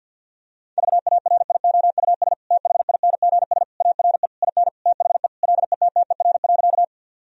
Morse code, heard as a keyed tone, about 50 wpm, 700 Hz. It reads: VARIOUS THINGS ARE IN THE LETTER.
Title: Sentences from Top 1000 Words - Review Part 4 - 50wpm